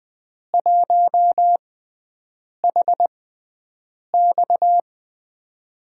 Morse code, heard as a keyed tone, about 20 wpm, spaced as for 7 wpm, 700 Hz. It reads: 1HX
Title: USA Call Signs - 6x Character Spacing - 20wpm